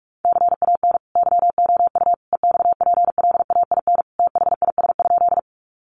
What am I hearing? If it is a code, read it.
Morse code, heard as a keyed tone, about 44 wpm, 700 Hz. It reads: CAN YOU EXPLAIN THIS?
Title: Sentences from Top 1100 Words 45wpm